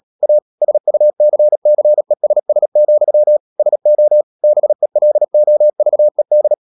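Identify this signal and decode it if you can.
Morse code, heard as a keyed tone, about 37 wpm, 600 Hz. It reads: A SUCCESS, SO BELOVED
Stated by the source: Morse code practice at 35–39 wpm sending English text of Ken Follett's novel Never, from CW med Gnister.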